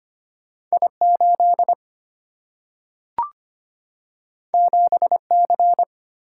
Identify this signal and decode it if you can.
Morse code, heard as a keyed tone, about 25 wpm, 700 Hz. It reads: I8 7C
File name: Two Letter-Number Combinations - 25wpm